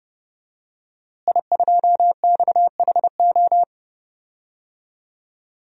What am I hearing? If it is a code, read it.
Morse code, heard as a keyed tone, about 30 wpm, 700 Hz. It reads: I2XHO
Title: DXCC-100 Call Signs - 5 Characters - Repeated 3x - 30wpm